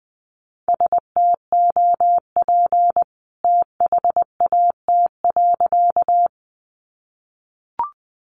Morse code, heard as a keyed tone, about 20 wpm, 700 Hz. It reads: STOP THAT.
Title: A Sound Of Thunder 20wpm